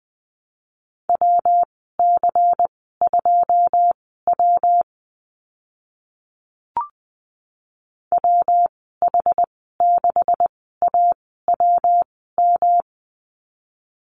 Morse code, heard as a keyed tone, about 20 wpm, spaced as for 15 wpm, 700 Hz. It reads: WC2W WH6AWM
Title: 128 Lesson 21g - Number 6 - Call Signs 20wpm